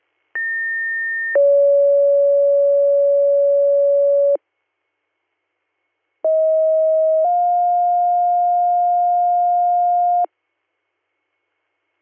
Two-tone page, heard as a one-second tone, then a three-second tone, about 1800 then 560 Hz, about 640 then 710 Hz. It is heard from 0.4 to 10.3 s.